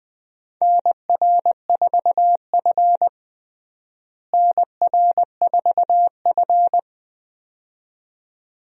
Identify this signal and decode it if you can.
Morse code, heard as a keyed tone, about 20 wpm, 700 Hz. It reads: NR4F NR4F
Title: USA Call Signs - 4 Characters - Repeated 2x - 20wpm